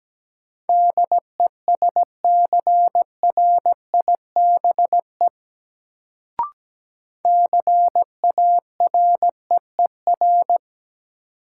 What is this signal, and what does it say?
Morse code, heard as a keyed tone, about 17 wpm, 700 Hz. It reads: DESCRIBE CAREER
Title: Sentences from Top 1000 Words - New Word Drill - 17wpm